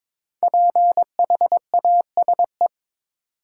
Morse code, heard as a keyed tone, about 22 wpm, 700 Hz. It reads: PHASE
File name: 5 Letter Words - First 3 Letters Given - 22wpm